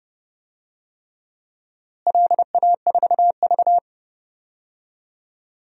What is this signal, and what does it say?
Morse code, heard as a keyed tone, about 30 wpm, 700 Hz. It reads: LA4V
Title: DXCC-100 Call Signs - 4 Characters - Repeated 3x - 30wpm